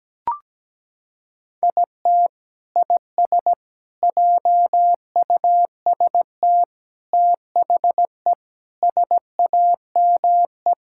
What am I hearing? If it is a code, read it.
Morse code, heard as a keyed tone, about 17 wpm, 700 Hz. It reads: IT IS JUST THE SAME
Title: Sentences from Top 1000 Words - Review Part 6 - 17wpm